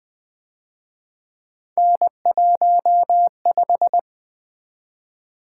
Morse code, heard as a keyed tone, about 24 wpm, 700 Hz. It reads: N15
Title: Speed-Racing - Three Letter-Number Combinations - 20wpm